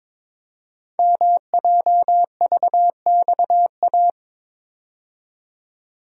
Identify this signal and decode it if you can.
Morse code, heard as a keyed tone, about 23 wpm, 700 Hz. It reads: MJVXA
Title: Code Groups - 5 Letters - 22wpm